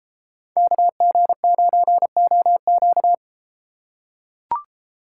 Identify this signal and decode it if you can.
Morse code, heard as a keyed tone, about 33 wpm, 700 Hz. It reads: KG9OQ